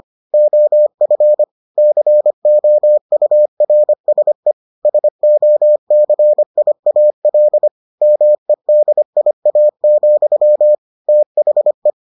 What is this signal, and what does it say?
Morse code, heard as a keyed tone, about 25 wpm, 600 Hz. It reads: OF COURSE SOCIAL MEDIA, THE